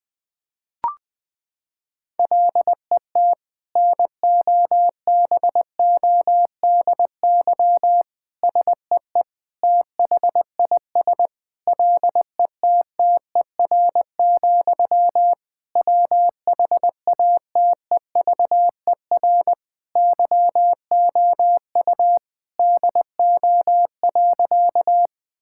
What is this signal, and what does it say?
Morse code, heard as a keyed tone, about 20 wpm, 700 Hz. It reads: LET NOBODY SEE THIS LETTER, WHATEVER YOU DO.
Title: Sentences from Top 1600 Words 20wpm